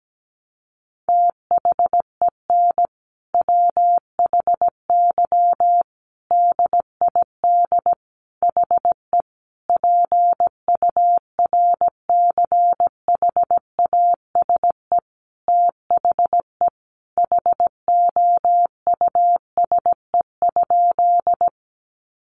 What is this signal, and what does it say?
Morse code, heard as a keyed tone, about 17 wpm, 700 Hz. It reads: THEN WHY DID HE PURCHASE THE HOUSE?
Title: Sentences from Top 800 Words 17wpm